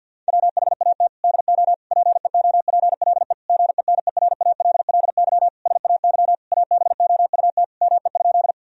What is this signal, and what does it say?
Morse code, heard as a keyed tone, about 50 wpm, 700 Hz. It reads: WHAT DO PEOPLE GENERALLY SAY ABOUT ME?
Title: Sentences from Top 1000 Words - Review Part 4 - 50wpm